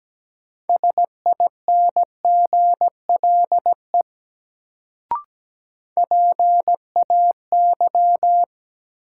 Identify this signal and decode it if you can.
Morse code, heard as a keyed tone, about 17 wpm, 700 Hz. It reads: SINGLE PAY
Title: Sentences from Top 700 Words - New Word Drill - 17wpm